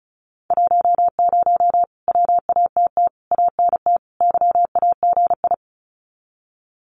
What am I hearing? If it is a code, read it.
Morse code, heard as a keyed tone, about 34 wpm, 700 Hz. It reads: 10 WATT ANT YAGI
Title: Common QSO Phrases Encore 35wpm